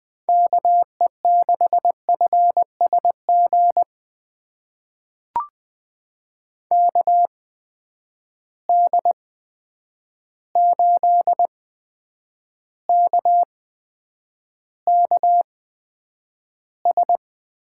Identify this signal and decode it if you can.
Morse code, heard as a keyed tone, about 20 wpm, spaced as for 9 wpm, 700 Hz. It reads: KE6FSG KD8KKS
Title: USA Call Signs - 8x Character Spacing - 20wpm